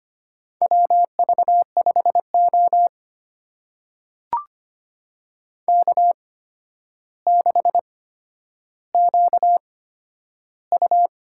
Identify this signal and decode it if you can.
Morse code, heard as a keyed tone, about 25 wpm, spaced as for 11 wpm, 700 Hz. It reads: WV5O K6QU
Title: USA Call Signs - 4 Characters - 8x Character Spacing - 25wpm